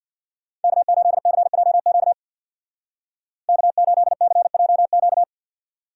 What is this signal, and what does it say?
Morse code, heard as a keyed tone, about 62 wpm, 700 Hz. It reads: K8CJQ K8CJQ
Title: Speed Racing - USA Call Signs 50wpm